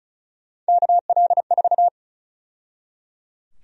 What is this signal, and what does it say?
Morse code, heard as a keyed tone, about 34 wpm, 700 Hz. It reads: KL4
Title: USA Call Sign Prefixes - 35wpm